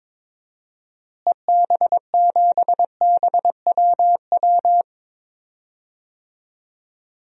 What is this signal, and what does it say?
Morse code, heard as a keyed tone, about 22 wpm, 700 Hz. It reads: EB7BWW
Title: DXCC-100 Call Signs - 6 Characters - Repeated 3x - 22wpm